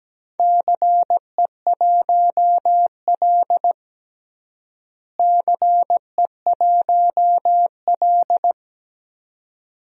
Morse code, heard as a keyed tone, about 17 wpm, 700 Hz. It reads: CE1L CE1L
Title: DXCC-100 Call Signs - 4 Characters - Repeated 2x - 17wpm